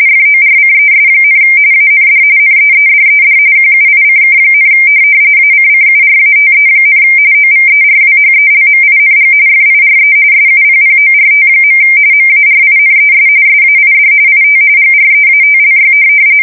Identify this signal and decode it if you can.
RTTY, about 45.45 baud. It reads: FORCES, AND NEEDED QUICK, ACCURATE, OVERWHELMING FIRE SUPPORT. MARINES SAID IT GREATLY
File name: fsk170 / USS-New-Jersey